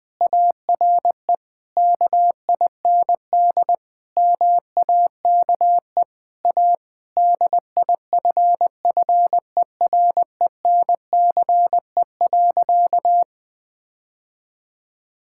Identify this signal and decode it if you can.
Morse code, heard as a keyed tone, about 20 wpm, 700 Hz. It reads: ARE KIND MAKE A DIFFERENCE.
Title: Sentences with 3-Word Phrases - Spoken Beginning - 20wpm